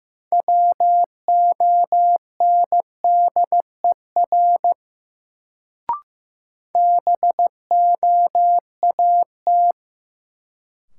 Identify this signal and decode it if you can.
Morse code, heard as a keyed tone, about 15 wpm, 700 Hz. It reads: WONDER BOAT
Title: Sentences from Top 500 Words - New Word Drill - 15wpm